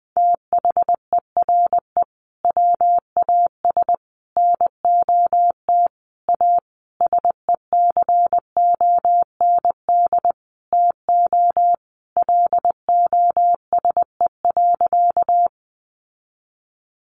Morse code, heard as a keyed tone, about 20 wpm, 700 Hz. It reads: THERE WAS NOT A SECOND TO LOSE.
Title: Sentences from Top 1100 Words 20wpm